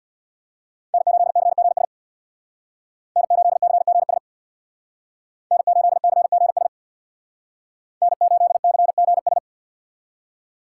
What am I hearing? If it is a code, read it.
Morse code, heard as a keyed tone, about 59 wpm, 700 Hz. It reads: N8CGS N8CGS N8CGS N8CGS
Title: Speed Racing - USA Call Signs 50wpm